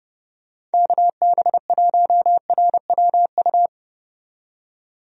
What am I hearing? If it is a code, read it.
Morse code, heard as a keyed tone, about 31 wpm, 700 Hz. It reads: KB1RWU
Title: USA Call Signs - 6 Characters - 30wpm